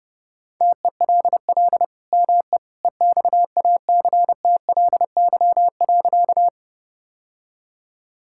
Morse code, heard as a keyed tone, about 30 wpm, 700 Hz. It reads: TELL ME EXACTLY.
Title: Sentences from Top 1000 Words - Review Part 6 - 30wpm